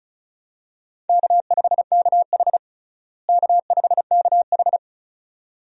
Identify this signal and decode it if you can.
Morse code, heard as a keyed tone, about 36 wpm, 700 Hz. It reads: K5KH K5KH
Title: USA Call Signs - 4 Characters - Repeated 2x - 35wpm